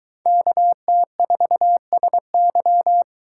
Morse code, heard as a keyed tone, about 23 wpm, 700 Hz. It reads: KT4SY